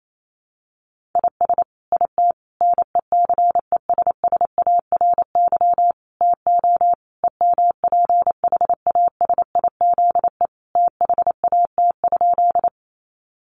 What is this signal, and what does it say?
Morse code, heard as a keyed tone, about 28 wpm, 700 Hz. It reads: IS IT NECESSARY TO EMPHASIZE THAT?
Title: Sentences from Top 1500 Words 28wpm